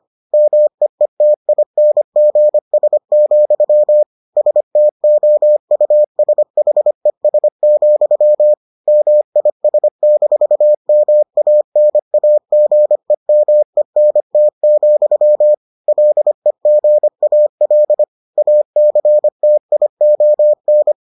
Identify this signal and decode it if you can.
Morse code, heard as a keyed tone, about 25 wpm, 600 Hz. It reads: MEETINGS, STOUSHES, MIS-MANAGEMENT, LEGAL ACTION